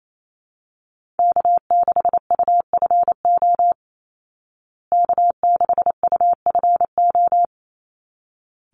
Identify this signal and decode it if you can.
Morse code, heard as a keyed tone, about 28 wpm, 700 Hz. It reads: K6UFO K6UFO
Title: Call Signs Repeated 3x - 28wpm